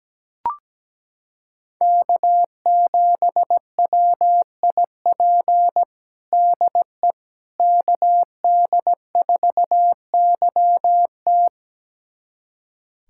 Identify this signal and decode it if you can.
Morse code, heard as a keyed tone, about 17 wpm, 700 Hz. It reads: K7WIP DE KD4YT
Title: USA Callsign DE Callsign - 5 Characters - 17wpm